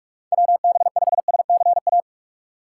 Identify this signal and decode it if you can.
Morse code, heard as a keyed tone, about 44 wpm, 700 Hz. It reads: WB5SCA
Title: USA Call Signs - 3x Character Spacing - 45wpm